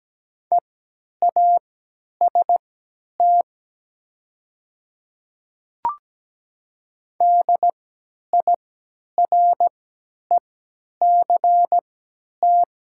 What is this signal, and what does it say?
Morse code, heard as a keyed tone, about 17 wpm, spaced as for 10 wpm, 700 Hz. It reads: EAST DIRECT
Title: 3x Character Spacing - Top 500 Words - 17wpm